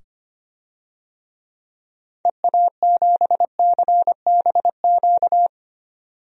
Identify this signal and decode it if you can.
Morse code, heard as a keyed tone, about 25 wpm, 700 Hz. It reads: EA7CBQ
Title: DXCC-100 Call Signs - 6 Characters - 25wpm